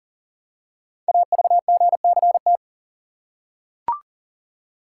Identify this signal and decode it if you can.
Morse code, heard as a keyed tone, about 39 wpm, 690 Hz. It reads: AVGCT
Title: Code Groups - 5 Letter:Numbers - 40wpm